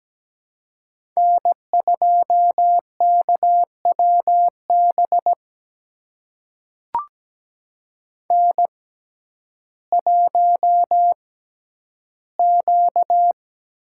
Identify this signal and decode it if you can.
Morse code, heard as a keyed tone, about 17 wpm, spaced as for 10 wpm, 700 Hz. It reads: N2KWB N1Q
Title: USA Call Signs - 6x Character Spacing - 17wpm